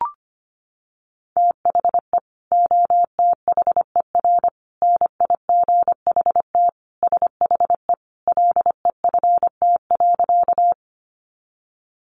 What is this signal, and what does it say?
Morse code, heard as a keyed tone, about 25 wpm, 700 Hz. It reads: THE OTHER NIGHT SHE LEFT.
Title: Sentences from Top 300 Words 25wpm